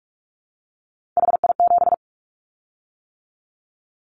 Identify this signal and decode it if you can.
Morse code, heard as a keyed tone, about 44 wpm, 700 Hz. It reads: HI7
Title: Callsign Prefixes for Top 100 DXCC - 45wpm